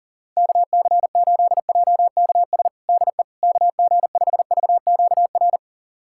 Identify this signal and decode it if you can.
Morse code, heard as a keyed tone, about 40 wpm, 700 Hz. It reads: KC8JKS DE KG5VQR
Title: USA Callsign DE Callsign - 6 Characters - 40wpm